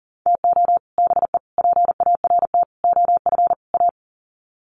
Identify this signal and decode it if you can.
Morse code, heard as a keyed tone, about 40 wpm, 700 Hz. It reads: TO BE PART OF A
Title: Sets of 5 Words Encore2 40wpm